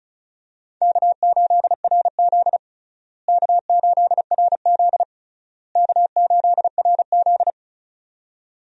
Morse code, heard as a keyed tone, about 34 wpm, 700 Hz. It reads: K8RZ K8RZ K8RZ
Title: USA Call Signs - 4 Characters - Repeated 3x - 35wpm